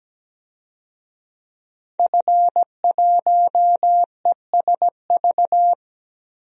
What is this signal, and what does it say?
Morse code, heard as a keyed tone, about 17 wpm, 700 Hz. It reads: F1ESV
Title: DXCC-100 Call Signs - 5 Characters - 17wpm